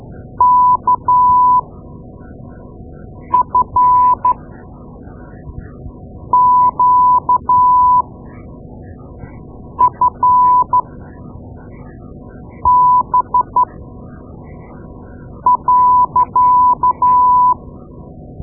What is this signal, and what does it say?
Morse code, heard as a keyed tone, about 13 wpm, spaced as for 4 wpm, 1000 Hz. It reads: KFQFB.
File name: Morse Lessons / Morse Lessons / Tape-2A